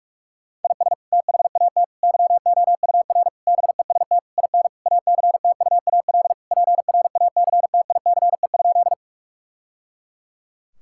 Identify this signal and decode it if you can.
Morse code, heard as a keyed tone, about 46 wpm, 700 Hz. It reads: IS THAT YOUR BEST IN ACTUAL PRACTICE?
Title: Sentences from Top 1000 Words - Review Part 5 - 45wpm